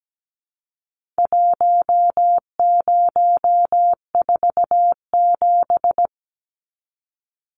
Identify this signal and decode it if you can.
Morse code, heard as a keyed tone, about 17 wpm, 700 Hz. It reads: 1047
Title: Numbers - Contest Style - 4-Digit 17wpm